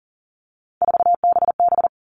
Morse code, heard as a keyed tone, about 40 wpm, 700 Hz. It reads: 4BB
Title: All Cut Numbers - 3-Digit 40wpm